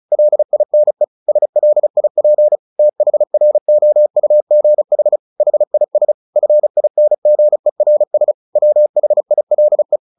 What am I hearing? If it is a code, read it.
Morse code, heard as a keyed tone, about 34 wpm, 600 Hz. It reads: LINE SLIP THROUGH HIS FINGERS WHILE
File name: the old man and the sea / the old man and the sea 1 35wpm